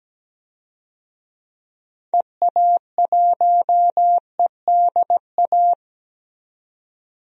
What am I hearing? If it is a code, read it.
Morse code, heard as a keyed tone, about 17 wpm, 700 Hz. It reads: EA1EDA
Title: DXCC-100 Call Signs - 6 Characters - 17wpm